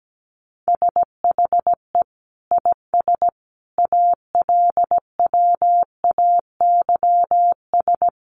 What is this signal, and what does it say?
Morse code, heard as a keyed tone, about 17 wpm, 700 Hz. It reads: SHE IS ALWAYS
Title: Sentences from Top 1200 Words 17wpm